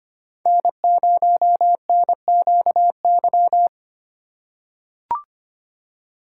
Morse code, heard as a keyed tone, about 25 wpm, 700 Hz. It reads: N0NQY
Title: USA Call Signs - 5 Characters - 3x Character Spacing - 25wpm